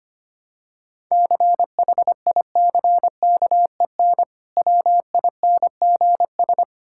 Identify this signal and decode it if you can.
Morse code, heard as a keyed tone, about 25 wpm, 700 Hz. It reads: CHICKEN WINGS
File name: practice / 25wpm / Difficult Words with Hints - 25wpm